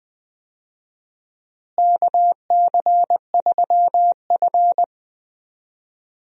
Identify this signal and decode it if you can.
Morse code, heard as a keyed tone, about 20 wpm, 700 Hz. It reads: KC3F
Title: USA Call Signs - 4 Characters - Repeated 3x - 20wpm